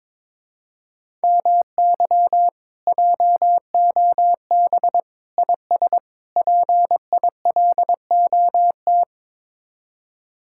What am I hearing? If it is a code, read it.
Morse code, heard as a keyed tone, about 22 wpm, 700 Hz. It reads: MY JOB IS PILOT
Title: Speed-Racing Common QSO Phrases - 22wpm